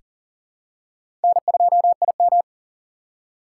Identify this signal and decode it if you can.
Morse code, heard as a keyed tone, about 40 wpm, 700 Hz. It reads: N2IM
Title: USA Call Signs - 4 Characters - 2x Character Spacing - 40wpm